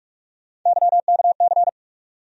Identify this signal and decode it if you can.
Morse code, heard as a keyed tone, about 46 wpm, 700 Hz. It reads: YKC
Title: USA Call Sign Postfixes - 3 Letters 45wpm